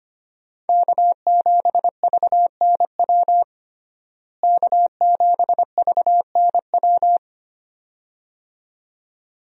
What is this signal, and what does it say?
Morse code, heard as a keyed tone, about 25 wpm, 700 Hz. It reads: K7VNW K7VNW
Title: USA Call Signs - 5 Characters - Repeated 2x - 25wpm